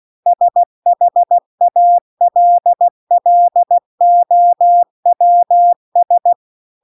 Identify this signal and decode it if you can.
Morse code, heard as a keyed tone, about 16 wpm, 700 Hz. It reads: SHALLOWS
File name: CW-The-Big-Twoheaded-River-2-16WPM